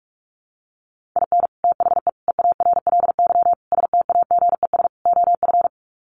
Morse code, heard as a keyed tone, about 44 wpm, 700 Hz. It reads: IN THE EARLY STAGES OF
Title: Sets of 5 Words Encore 45wpm